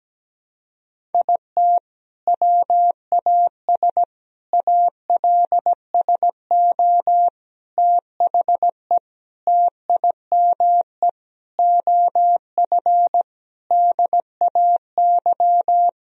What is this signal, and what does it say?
Morse code, heard as a keyed tone, about 17 wpm, 700 Hz. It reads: IT WAS ALSO THE TIME OF DAY
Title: Sentences from Top 1000 Words - Review Part 1 - 17wpm